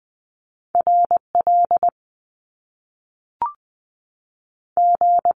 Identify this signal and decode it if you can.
Morse code, heard as a keyed tone, about 20 wpm, 700 Hz. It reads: R L G
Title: Top Two Letter Combinations in Words 20wpm